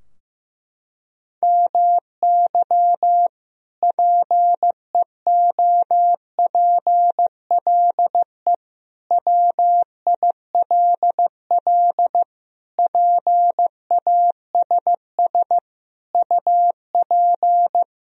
Morse code, heard as a keyed tone, about 15 wpm, 700 Hz. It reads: MY PEOPLE WILL PASS UP
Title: Sentences from Top 1000 Words - Review Part 2 - 15wpm